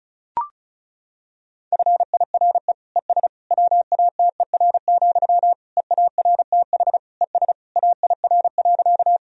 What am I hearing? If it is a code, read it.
Morse code, heard as a keyed tone, about 34 wpm, 700 Hz. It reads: FIRE ES WATER, EARTH ES AIR.
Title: Sentences from Top 400 Words and Abbreviations 35wpm